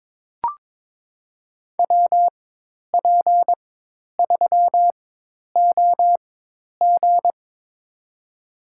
Morse code, heard as a keyed tone, about 22 wpm, spaced as for 10 wpm, 700 Hz. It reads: WP3OG
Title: USA Call Signs - 5 Characters - 4x Character Spacing - 22wpm